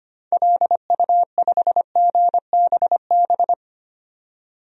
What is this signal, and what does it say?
Morse code, heard as a keyed tone, about 25 wpm, 700 Hz. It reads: LU5GBB
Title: DXCC-100 Call Signs - 6 Characters - 25wpm